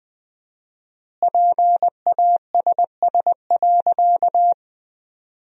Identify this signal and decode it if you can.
Morse code, heard as a keyed tone, about 20 wpm, 700 Hz. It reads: PASS.
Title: 4x Word-Spacing - Sentences from Top 400 Words - 20wpm